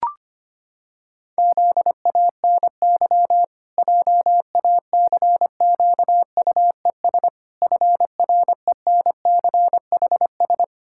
Morse code, heard as a keyed tone, about 25 wpm, 700 Hz. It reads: ZANY JACQUES FRENCHS
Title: Pangram Challenge v2 - 25wpm